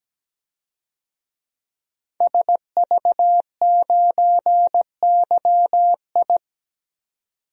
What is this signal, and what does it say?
Morse code, heard as a keyed tone, about 17 wpm, 700 Hz. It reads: SV9YI
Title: DXCC-100 Call Signs - 5 Characters - 17wpm